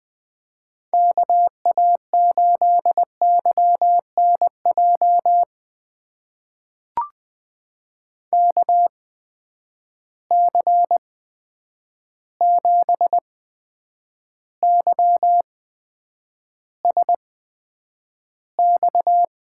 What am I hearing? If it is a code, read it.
Morse code, heard as a keyed tone, about 20 wpm, spaced as for 9 wpm, 700 Hz. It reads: KA8YNJ KC7YSX
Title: USA Call Signs - 6 Characters - 8x Character Spacing - 20wpm